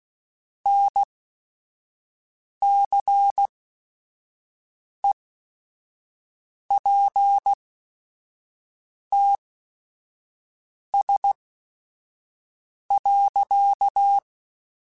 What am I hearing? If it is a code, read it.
Morse code, heard as a keyed tone, about 16 wpm, spaced as for 5 wpm, 800 Hz. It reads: NCEPTS.
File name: Technical Subjects